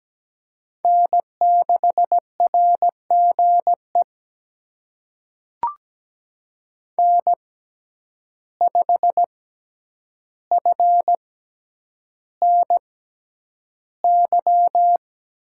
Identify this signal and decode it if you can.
Morse code, heard as a keyed tone, about 17 wpm, spaced as for 9 wpm, 700 Hz. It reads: N6RGE N5FNY
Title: USA Call Signs - 5 Characters - 6x Character Spacing - 17wpm